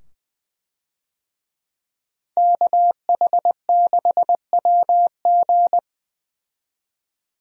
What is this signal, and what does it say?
Morse code, heard as a keyed tone, about 20 wpm, 700 Hz. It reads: KH6WG
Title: DXCC-100 Call Signs - 5 Characters - 20wpm